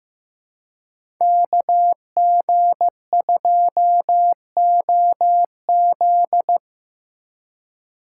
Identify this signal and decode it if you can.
Morse code, heard as a keyed tone, about 17 wpm, 700 Hz. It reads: KG2OZ
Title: Speed Racing - USA 2x2 Call Signs 15wpm